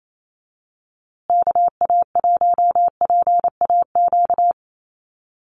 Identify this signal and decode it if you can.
Morse code, heard as a keyed tone, about 28 wpm, 700 Hz. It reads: KA1PAQ
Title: USA Call Signs Repeated 3x 28wpm